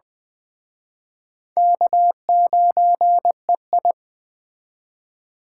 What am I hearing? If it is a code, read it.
Morse code, heard as a keyed tone, about 20 wpm, 700 Hz. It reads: K9EI